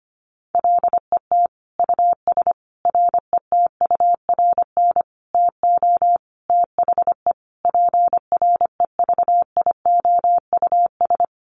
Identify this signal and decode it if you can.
Morse code, heard as a keyed tone, about 25 wpm, 700 Hz. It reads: LET US RETURN TO THE PREVIOUS
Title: Sentences from Top 1500 Words 25wpm